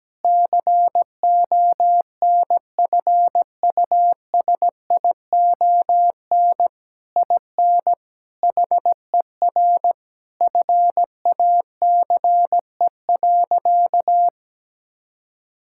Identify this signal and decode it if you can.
Morse code, heard as a keyed tone, about 17 wpm, 700 Hz. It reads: CONFUSION IN HER FACE.
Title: Sentences from Top 1600 Words 17wpm